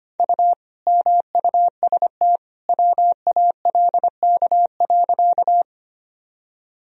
Morse code, heard as a keyed tone, about 25 wpm, 700 Hz. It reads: U MUST WALK.
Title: Sentences from Top 300 Words and Abbreviations 25wpm